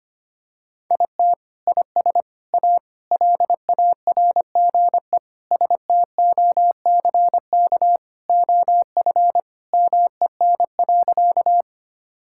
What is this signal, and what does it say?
Morse code, heard as a keyed tone, about 25 wpm, 700 Hz. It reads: IT IS A LARGE STOCK OF MEN.
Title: Sentences from Top 1000 Words - Review Part 5 - 25wpm